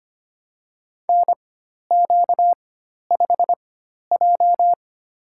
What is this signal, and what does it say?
Morse code, heard as a keyed tone, about 25 wpm, spaced as for 12 wpm, 700 Hz. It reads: NQ5J
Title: USA Call Signs - 4x Character Spacing - 25wpm